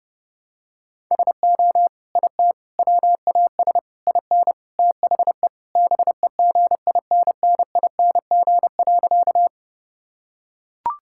Morse code, heard as a keyed tone, about 30 wpm, 700 Hz. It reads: SO IT WAS IN THE BEGINNING.
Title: Sentences from Top 1000 Words - Review Part 5 - 30wpm